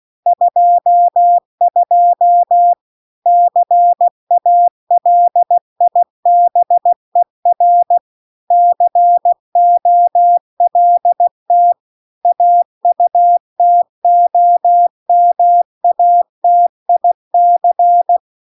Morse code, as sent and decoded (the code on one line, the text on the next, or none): ..--- ..--- / -.-. .- .-.. .. -... .-. / -.-. --- .-.. - / .- ..- - --- -- .- - .. -.-.
22 CALIBER COLT AUTOMATIC